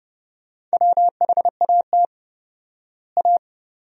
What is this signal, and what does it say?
Morse code, heard as a keyed tone, about 30 wpm, 700 Hz. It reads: WHAT A
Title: 4x Word-Spacing - Sentences from Top 400 Words - 30wpm